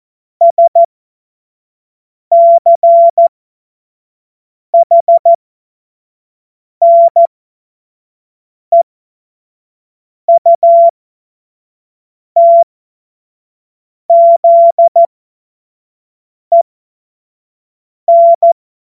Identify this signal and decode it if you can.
Morse code, heard as a keyed tone, about 14 wpm, spaced as for 5 wpm, 680 Hz. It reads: SCHNEUTZEN
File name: Morsekurs / Witze / Witz 10.Oktober 2023 25 F70